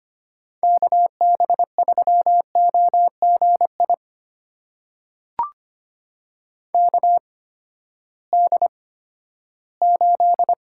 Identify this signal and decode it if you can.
Morse code, heard as a keyed tone, about 25 wpm, spaced as for 14 wpm, 700 Hz. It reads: KB3OGI KD8
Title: USA Call Signs - 6 Characters - 8x Character Spacing - 25wpm